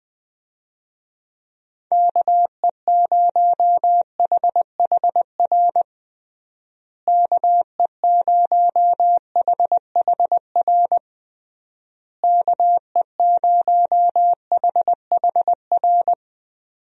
Morse code, heard as a keyed tone, about 21 wpm, 700 Hz. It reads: KE0HHR KE0HHR KE0HHR
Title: USA Call Signs - 6 Characters - Repeated 3x - 20wpm